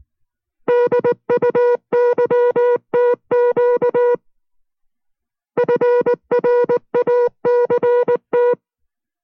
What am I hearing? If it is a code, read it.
Morse code, heard as a keyed tone, about 19 wpm, 480 Hz. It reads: DUYTQ FRACT